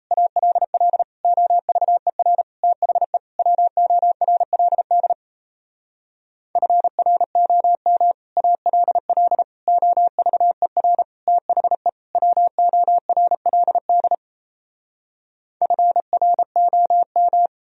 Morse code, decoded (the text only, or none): ALL OVER THE WORLD FROM ALL OVER THE WORLD FROM